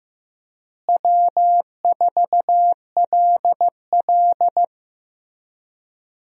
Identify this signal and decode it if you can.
Morse code, heard as a keyed tone, about 16 wpm, 700 Hz. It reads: W4LL
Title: USA Call Signs - 4 Characters - Repeated 3x - 15wpm